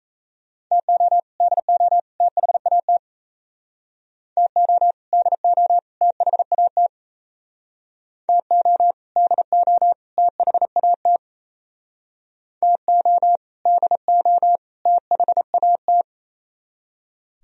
Morse code, heard as a keyed tone, about 36 wpm, 700 Hz. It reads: TO DO THAT TO DO THAT TO DO THAT TO DO THAT